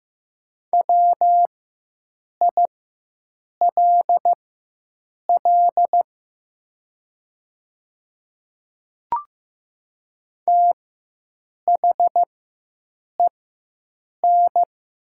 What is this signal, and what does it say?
Morse code, heard as a keyed tone, about 15 wpm, spaced as for 7 wpm, 700 Hz. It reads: WILL THEN
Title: 4x Character Spacing - Top 200 Words - 15wpm